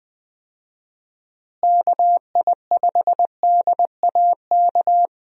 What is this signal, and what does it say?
Morse code, heard as a keyed tone, about 21 wpm, 700 Hz. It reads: KI5DAK
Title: USA Call Signs - 6 Characters - Repeated 3x - 20wpm